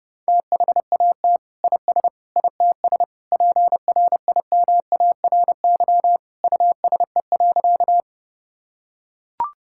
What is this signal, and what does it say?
Morse code, heard as a keyed tone, about 31 wpm, 700 Hz. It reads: THAT IS ITS PRIMARY USE.
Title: Sentences from Top 1000 Words - Review Part 5 - 30wpm